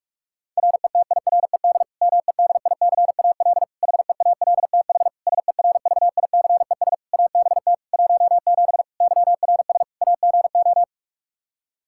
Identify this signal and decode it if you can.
Morse code, heard as a keyed tone, about 46 wpm, 700 Hz. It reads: RETIRED MEDICAL HEALTH SERVICES ABT 17 YRS AGO
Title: K1USN SST QSO Practice 45wpm